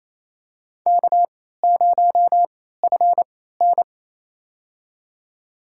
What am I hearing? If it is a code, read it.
Morse code, heard as a keyed tone, about 28 wpm, spaced as for 16 wpm, 700 Hz. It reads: K0FN
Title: USA Call Signs - 4 Characters - 3x Character Spacing - 28wpm